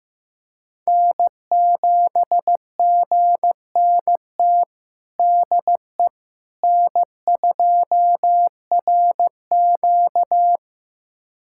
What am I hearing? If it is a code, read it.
Morse code, heard as a keyed tone, about 15 wpm, 700 Hz. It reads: N7GNT DE N2RQ
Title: USA Callsign DE Callsign 15wpm